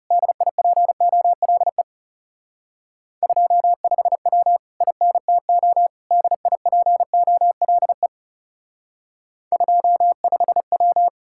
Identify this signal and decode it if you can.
Morse code, heard as a keyed tone, about 37 wpm, 700 Hz. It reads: DIPOLE 25W INTO DIPOLE 25W
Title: Speed-Racing Common QSO Phrases Encore - 30wpm